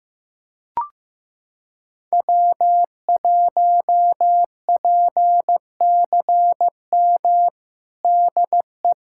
Morse code, heard as a keyed tone, about 16 wpm, 700 Hz. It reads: W1PCM DE
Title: USA Callsign DE Callsign - 5 Characters - 15wpm